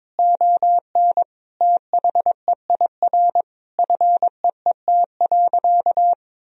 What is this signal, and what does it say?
Morse code, heard as a keyed tone, about 22 wpm, 700 Hz. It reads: ON THEIR FEET.
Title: Sentences from Top 1000 Words - Review Part 3 - 22wpm